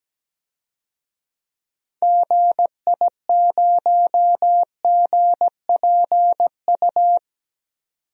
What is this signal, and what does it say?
Morse code, heard as a keyed tone, about 17 wpm, 700 Hz. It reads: GI0GPU
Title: DXCC-100 Call Signs - 6 Characters - 17wpm